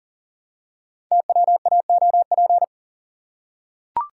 Morse code, heard as a keyed tone, about 42 wpm, 700 Hz. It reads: TWAOP
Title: Code Groups - 5 Letters - 40wpm